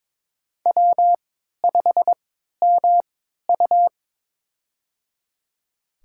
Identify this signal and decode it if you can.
Morse code, heard as a keyed tone, about 22 wpm, spaced as for 13 wpm, 700 Hz. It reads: W5MU
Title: USA Call Signs - 3x Character Spacing - 22wpm